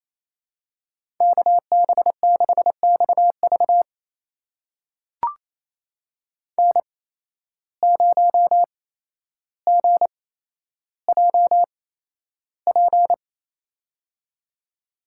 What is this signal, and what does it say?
Morse code, heard as a keyed tone, about 27 wpm, spaced as for 12 wpm, 700 Hz. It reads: KB6XV N0GJP
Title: USA Call Signs - 5 Characters - 8x Character Spacing - 28wpm